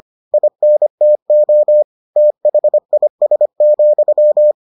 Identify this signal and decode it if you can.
Morse code, heard as a keyed tone, about 25 wpm, 600 Hz. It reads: INTO THIS,